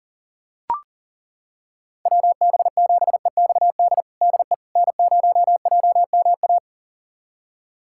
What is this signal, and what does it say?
Morse code, heard as a keyed tone, about 40 wpm, 700 Hz. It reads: WB7EXD DE N0JMA
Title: USA Callsign DE Callsign 40wpm